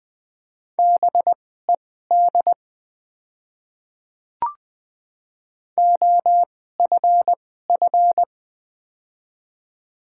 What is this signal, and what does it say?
Morse code, heard as a keyed tone, about 20 wpm, spaced as for 15 wpm, 690 Hz. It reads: BED OFF
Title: 2x Character Spacing - Top 500 Words - 20wpm